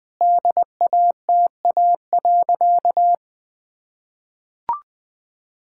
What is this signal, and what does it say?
Morse code, heard as a keyed tone, about 20 wpm, 700 Hz. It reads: DATA.
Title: Sentences with 7-Word Phrases - 20wpm